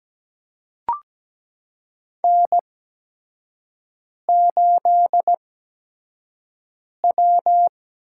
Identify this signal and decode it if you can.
Morse code, heard as a keyed tone, about 17 wpm, spaced as for 5 wpm, 700 Hz. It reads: N8W
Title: USA Call Signs - 5 Characters - 8x Character Spacing - 17wpm